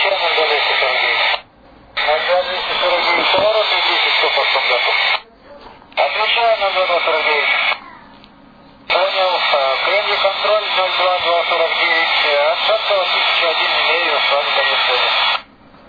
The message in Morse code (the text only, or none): none